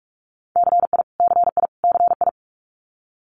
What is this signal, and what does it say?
Morse code, heard as a keyed tone, about 46 wpm, 700 Hz. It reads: CI CI CI
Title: Multi-letter Phonemes Repeated 3x 45wpm